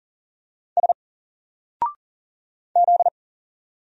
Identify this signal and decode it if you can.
Morse code, heard as a keyed tone, about 40 wpm, 700 Hz. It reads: S Z
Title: Single Letter-Number 40wpm